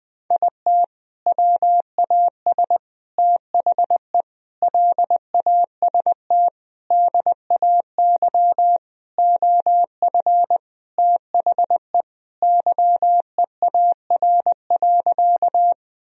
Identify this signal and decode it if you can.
Morse code, heard as a keyed tone, about 20 wpm, 700 Hz. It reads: IT WAS THE LAST DAY OF THE YEAR.
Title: Sentences from Top 1000 Words - Review Part 1 - 20wpm